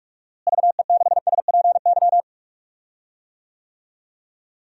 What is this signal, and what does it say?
Morse code, heard as a keyed tone, about 46 wpm, 700 Hz. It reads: VE6SPY
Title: DXCC-100 Call Signs - 6 Characters - Repeated 2x - 45wpm